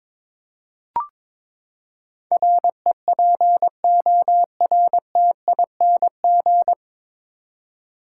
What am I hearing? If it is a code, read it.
Morse code, heard as a keyed tone, about 22 wpm, 700 Hz. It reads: REPORTING
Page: Morse Code Ninja practice 22wpm